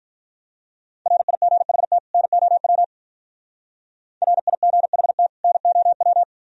Morse code, heard as a keyed tone, about 42 wpm, 700 Hz. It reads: RIGHT NOW RIGHT NOW
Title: Speed-Racing - Sets of 2 Words 35wpm